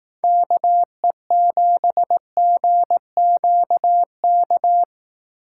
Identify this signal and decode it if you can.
Morse code, heard as a keyed tone, about 19 wpm, 700 Hz. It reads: KE7GQK
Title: Speed Racing - USA Call Signs 15wpm